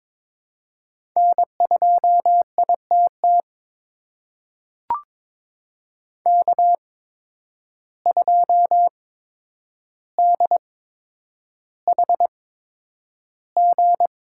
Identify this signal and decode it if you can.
Morse code, heard as a keyed tone, about 22 wpm, spaced as for 9 wpm, 700 Hz. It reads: N2ITT K2DHG
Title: USA Call Signs - 8x Character Spacing - 22wpm